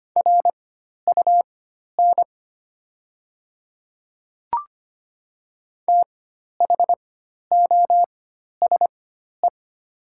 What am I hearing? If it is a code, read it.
Morse code, heard as a keyed tone, about 25 wpm, spaced as for 12 wpm, 700 Hz. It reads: RUN THOSE
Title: 4x Character Spacing - Top 500 Words - 25wpm